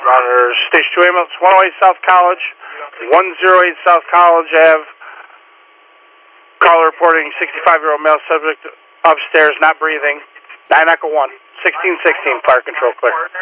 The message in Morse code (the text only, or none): none